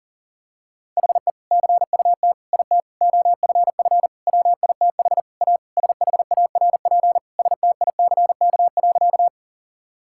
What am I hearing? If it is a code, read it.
Morse code, heard as a keyed tone, about 40 wpm, 700 Hz. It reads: HE CUT IT OFF WITH A SHARP STICK.
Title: Sentences from Top 1600 Words 40wpm